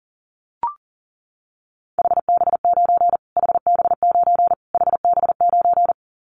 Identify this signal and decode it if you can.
Morse code, heard as a keyed tone, about 37 wpm, 700 Hz. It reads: HB9 HB9 HB9
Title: Callsign Prefixes for Top 25 DXCC - Repeated 3x - 40wpm